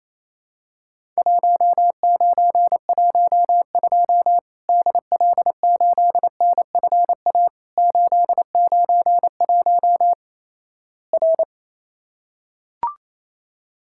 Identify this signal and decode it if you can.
Morse code, heard as a keyed tone, about 28 wpm, 700 Hz. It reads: 1912 DL8NFA 891 R
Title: Worked All Europe DX Contest - QTC Exchange - 28wpm